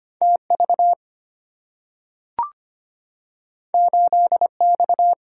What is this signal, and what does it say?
Morse code, heard as a keyed tone, about 25 wpm, 700 Hz. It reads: TV 8X
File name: Two Letter-Number Combinations - 25wpm